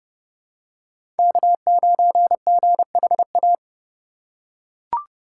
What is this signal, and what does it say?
Morse code, heard as a keyed tone, about 29 wpm, 700 Hz. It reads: K9GHA
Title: USA Call Signs - 3x Character Spacing - 30wpm